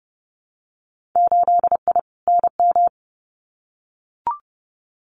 Morse code, heard as a keyed tone, about 30 wpm, 700 Hz. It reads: 8I NM
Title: Winter Field Day Exchanges Encore 30wpm